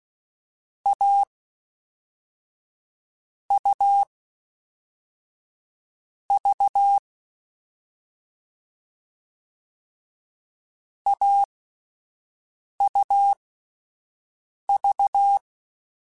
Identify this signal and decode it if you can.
Morse code, heard as a keyed tone, about 16 wpm, spaced as for 4 wpm, 800 Hz. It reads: AUV AUV
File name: Difficult Combos / Difficult Combos-mp3s / AUV ShortRun @5